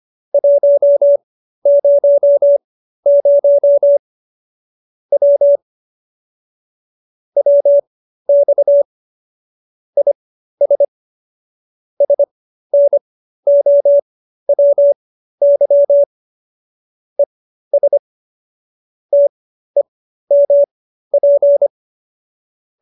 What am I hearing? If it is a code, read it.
Morse code, heard as a keyed tone, about 25 wpm, spaced as for 13 wpm, 570 Hz. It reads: 100 W WX IS SNOWY ES TEMP